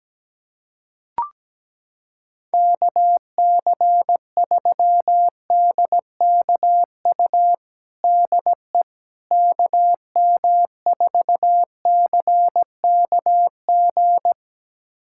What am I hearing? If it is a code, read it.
Morse code, heard as a keyed tone, about 17 wpm, 700 Hz. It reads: KC3DKU DE KM4CKG
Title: USA Callsign DE Callsign 17wpm